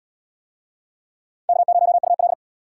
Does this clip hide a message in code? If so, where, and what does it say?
Yes: Morse code from 1.5 to 2.3 s, heard as a keyed tone, about 65 wpm, 700 Hz. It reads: K0UL